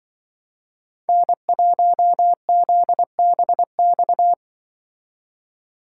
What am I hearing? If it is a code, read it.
Morse code, heard as a keyed tone, about 25 wpm, 700 Hz. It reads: N1ZBX